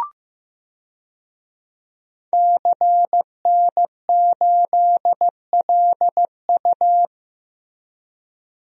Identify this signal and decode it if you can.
Morse code, heard as a keyed tone, about 15 wpm, 700 Hz. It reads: CN8LU
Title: DXCC-100 Call Signs - 5 Characters - Repeated 2x - 15wpm